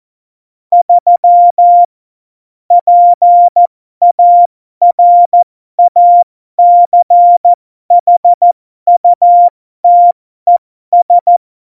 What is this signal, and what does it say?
Morse code, heard as a keyed tone, about 14 wpm, spaced as for 12 wpm, 700 Hz. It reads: 3 PARACHUTES
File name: Morsekurs / Witze